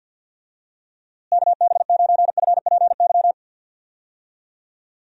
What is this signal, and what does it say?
Morse code, heard as a keyed tone, about 49 wpm, 700 Hz. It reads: KB9FPY
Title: USA Call Signs - 6 Characters - Repeated 3x - 50wpm